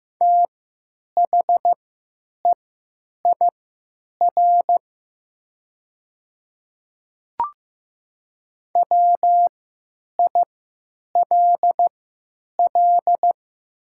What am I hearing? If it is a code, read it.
Morse code, heard as a keyed tone, about 15 wpm, spaced as for 9 wpm, 690 Hz. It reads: THEIR WILL
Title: 3x Character Spacing - Top 100 Words - 15wpm